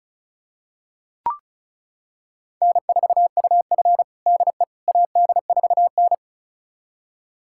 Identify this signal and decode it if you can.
Morse code, heard as a keyed tone, about 36 wpm, 700 Hz. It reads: N4UF DE AD4N
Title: USA Callsign DE Callsign - 4 Characters - 35wpm